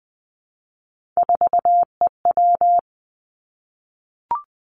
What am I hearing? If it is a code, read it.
Morse code, heard as a keyed tone, about 20 wpm, 700 Hz. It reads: 4EW
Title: Three Challenging Letter-Number Combinations 20wpm